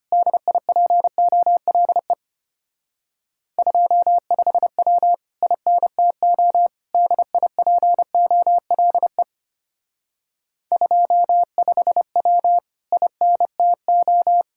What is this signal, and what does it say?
Morse code, heard as a keyed tone, about 31 wpm, 700 Hz. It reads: DIPOLE 25W INTO DIPOLE 25W INTO
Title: Speed-Racing Common QSO Phrases Encore - 25wpm